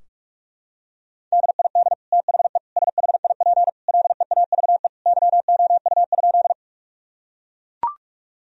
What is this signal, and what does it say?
Morse code, heard as a keyed tone, about 42 wpm, 700 Hz. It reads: DID THE SHIP LEAVE YOU?
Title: Sentences from Top 1000 Words - Review Part 2 - 45wpm